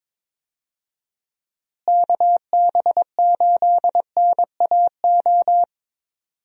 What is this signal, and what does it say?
Morse code, heard as a keyed tone, about 22 wpm, 700 Hz. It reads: KB8NAO
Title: USA Call Signs - 6 Characters - Repeated 2x - 22wpm